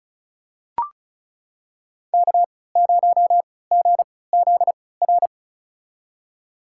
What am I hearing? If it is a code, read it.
Morse code, heard as a keyed tone, about 34 wpm, spaced as for 20 wpm, 700 Hz. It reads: K0GZR
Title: USA Call Signs - 3x Character Spacing - 35wpm